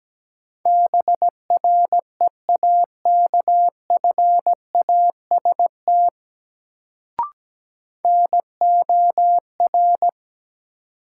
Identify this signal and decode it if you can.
Morse code, heard as a keyed tone, about 17 wpm, 700 Hz. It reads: BREAKFAST NOR
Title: Sentences from Top 1500 Words - New Word Drill - 17wpm